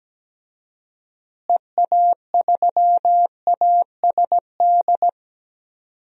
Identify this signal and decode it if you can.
Morse code, heard as a keyed tone, about 17 wpm, 700 Hz. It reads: EA3ASD
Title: DXCC-100 Call Signs - 6 Characters - Repeated 3x - 17wpm